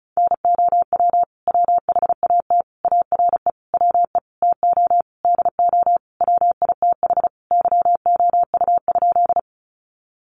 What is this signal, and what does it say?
Morse code, heard as a keyed tone, about 34 wpm, 700 Hz. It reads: NOW WHAT ARE WE TO DO WITH YOU?
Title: Sentences from Top 100 Words 35wpm